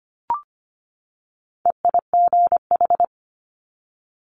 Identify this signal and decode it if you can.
Morse code, heard as a keyed tone, about 25 wpm, 700 Hz. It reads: EIGH
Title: Multi-letter Phonemes 25wpm